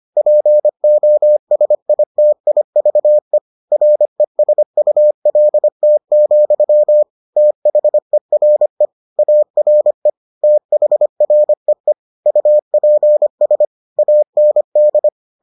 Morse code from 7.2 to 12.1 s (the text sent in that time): THERE ARE THREE